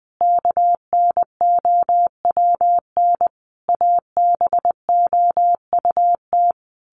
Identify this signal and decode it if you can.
Morse code, heard as a keyed tone, about 19 wpm, 700 Hz. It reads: KNOWN ABOUT
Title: Sentences from Top 1200 Words 20wpm